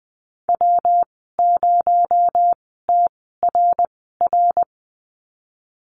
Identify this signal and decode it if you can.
Morse code, heard as a keyed tone, about 20 wpm, spaced as for 15 wpm, 700 Hz. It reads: W0TRR
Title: 238 Lesson 38g - Letter q - Call Signs 20wpm